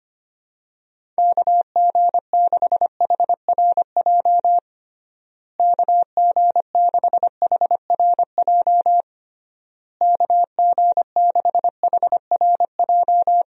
Morse code, heard as a keyed tone, about 25 wpm, 700 Hz. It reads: KG6HRJ KG6HRJ KG6HRJ
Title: USA Call Signs - 6 Characters - Repeated 3x - 25wpm